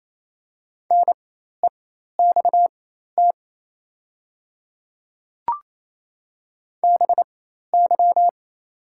Morse code, heard as a keyed tone, about 28 wpm, spaced as for 13 wpm, 700 Hz. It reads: NEXT BY
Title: 4x Character Spacing - Top 300 Words - 28wpm